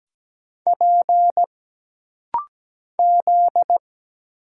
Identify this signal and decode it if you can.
Morse code, heard as a keyed tone, about 17 wpm, 700 Hz. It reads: P Z
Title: Single Letter-Number Alt - Rapid-Fire 17wpm